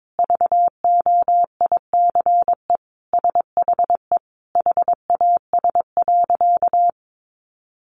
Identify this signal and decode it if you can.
Morse code, heard as a keyed tone, about 22 wpm, 700 Hz. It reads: VOICE SHE HAS.
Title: Sentences from Top 500 Words 22wpm